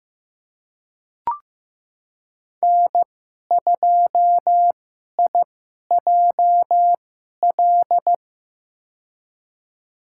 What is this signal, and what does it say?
Morse code, heard as a keyed tone, about 15 wpm, spaced as for 11 wpm, 700 Hz. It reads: N2IJL
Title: USA Call Signs - 5 Characters - 2x Character Spacing - 15wpm